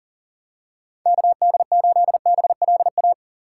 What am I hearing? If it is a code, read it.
Morse code, heard as a keyed tone, about 40 wpm, 700 Hz. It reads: KD8BLA
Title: USA Call Signs - 2x Character Spacing - 40wpm